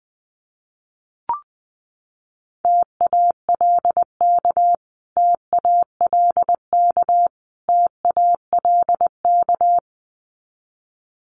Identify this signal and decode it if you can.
Morse code, heard as a keyed tone, about 20 wpm, 700 Hz. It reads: TALK TALK TALK
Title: Top 500 Words Repeated 20wpm